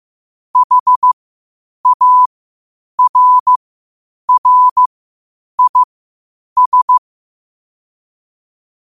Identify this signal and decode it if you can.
Morse code, heard as a keyed tone, about 15 wpm, spaced as for 8 wpm, 1000 Hz. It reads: HARRIS
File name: qotdfiles / 8wpm